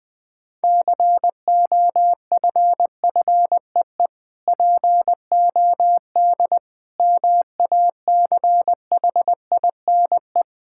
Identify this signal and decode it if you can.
Morse code, heard as a keyed tone, about 20 wpm, 700 Hz. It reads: COFFEE POD MACHINE
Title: Difficult Words - Speed-Racing with Hints - 20wpm